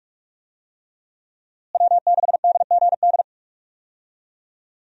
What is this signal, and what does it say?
Morse code, heard as a keyed tone, about 41 wpm, 700 Hz. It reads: W6DGD
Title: USA Call Signs - 5 Characters - Repeated 3x - 45wpm